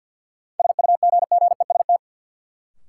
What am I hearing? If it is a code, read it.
Morse code, heard as a keyed tone, about 53 wpm, 700 Hz. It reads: SUGGEST
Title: Sentences from Top 1300 Words - New Word Drill Alt - 50wpm